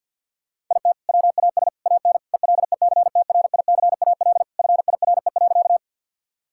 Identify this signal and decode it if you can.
Morse code, heard as a keyed tone, about 52 wpm, 700 Hz. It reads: IT WAS AN ELECTRICAL FIRE.
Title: Sentences from Top 1000 Words - Review Part 6 - 50wpm